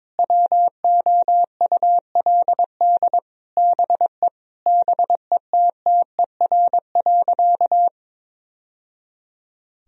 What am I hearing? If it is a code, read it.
Morse code, heard as a keyed tone, about 22 wpm, 700 Hz. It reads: WOULD BE BETTER.
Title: Sentences from Top 1000 Words - Review Part 1 - 22wpm